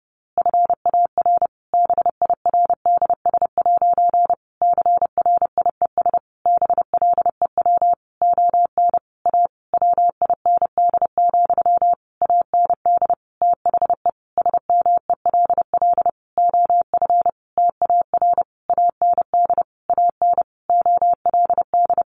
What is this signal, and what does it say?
Morse code, heard as a keyed tone, about 30 wpm, 700 Hz. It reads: FAR BIRDS' CRIES BLEW ON A WIND, AND THE SMELL OF TAR AND AN OLD
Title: A Sound Of Thunder 30wpm